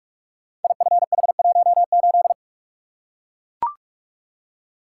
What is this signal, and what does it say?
Morse code, heard as a keyed tone, about 44 wpm, 700 Hz. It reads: IFH18
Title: Code Groups - 5 Letter:Numbers - 45wpm